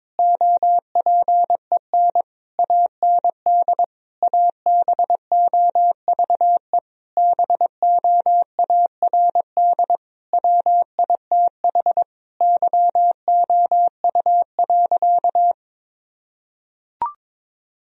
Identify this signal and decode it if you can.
Morse code, heard as a keyed tone, about 22 wpm, 700 Hz. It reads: OPEN AND ABOVE BOARD WITH YOU.
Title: Sentences from Top 1000 Words - Review Part 4 - 22wpm